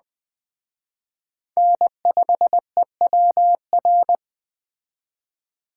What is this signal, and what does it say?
Morse code, heard as a keyed tone, about 20 wpm, 700 Hz. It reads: N5EWR
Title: Speed Racing - USA 1x3 Call Signs 17wpm